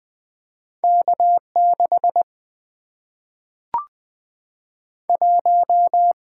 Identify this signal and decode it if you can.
Morse code, heard as a keyed tone, about 20 wpm, 700 Hz. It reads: K 6 1
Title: Single-Letter Single-Number 20wpm